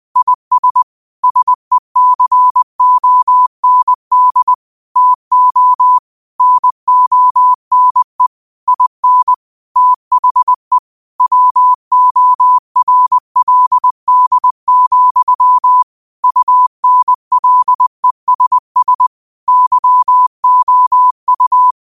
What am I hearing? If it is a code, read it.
Morse code, heard as a keyed tone, about 20 wpm, 1000 Hz. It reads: IS SECOND TO NONE IN THE WORLD, UNLESS YOU
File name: qotdfiles / 20wpm